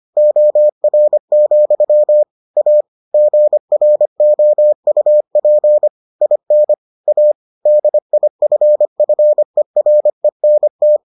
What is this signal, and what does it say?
Morse code, heard as a keyed tone, about 25 wpm, 600 Hz. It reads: OR, A GROUP IN A DIFFERENT